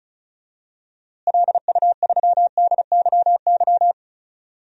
Morse code, heard as a keyed tone, about 34 wpm, 700 Hz. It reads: LU3DYY
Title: DXCC-100 Call Signs - 6 Characters - Repeated 2x - 35wpm